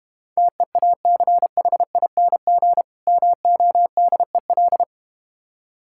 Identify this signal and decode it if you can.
Morse code, heard as a keyed tone, about 32 wpm, 700 Hz. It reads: TEACHING MODEL